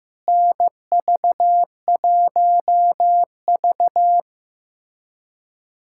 Morse code, heard as a keyed tone, about 15 wpm, 700 Hz. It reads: NV1V
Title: USA Callsign DE Callsign - 4 Characters - 15wpm